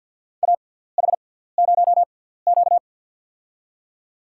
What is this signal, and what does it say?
Morse code, heard as a keyed tone, about 52 wpm, spaced as for 17 wpm, 700 Hz. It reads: AH0Q
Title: USA Call Signs - 4 Characters - 6x Character Spacing - 50wpm